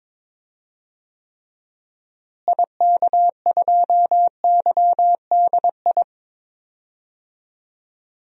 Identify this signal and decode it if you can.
Morse code, heard as a keyed tone, about 22 wpm, 700 Hz. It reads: IK2YDI
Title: DXCC-100 Call Signs - 6 Characters - Repeated 3x - 22wpm